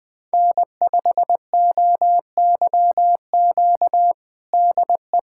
Morse code, heard as a keyed tone, about 20 wpm, 700 Hz. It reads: N5OYQ DE
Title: USA Callsign DE Callsign 20wpm